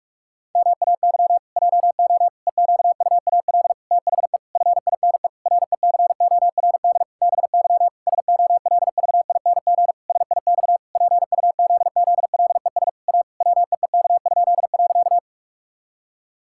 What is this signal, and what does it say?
Morse code, heard as a keyed tone, about 46 wpm, 700 Hz. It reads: MAY JO EQUAL THE FINE RECORD BY SOLVING SIX PUZZLES A WEEK?.
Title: Pangram Challenge v2 - 45wpm